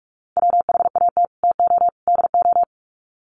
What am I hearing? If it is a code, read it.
Morse code, heard as a keyed tone, about 44 wpm, 700 Hz. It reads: WHAT TO DO
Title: Sets of 3 Words 45wpm